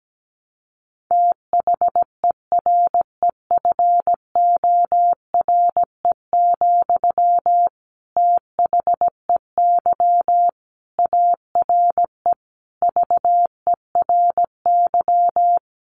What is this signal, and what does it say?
Morse code, heard as a keyed tone, about 17 wpm, 700 Hz. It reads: THEREFORE, THEY ARE VERY
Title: Sentences from Top 1200 Words 17wpm